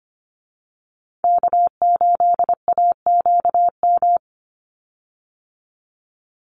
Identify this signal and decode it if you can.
Morse code, heard as a keyed tone, about 25 wpm, 700 Hz. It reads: K8AQM
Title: Call Signs Repeated 3x - 25wpm